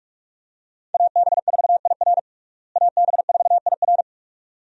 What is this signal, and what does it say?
Morse code, heard as a keyed tone, about 41 wpm, 700 Hz. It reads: AB4IR AB4IR
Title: USA Call Signs - 5 Characters - Repeated 2x - 45wpm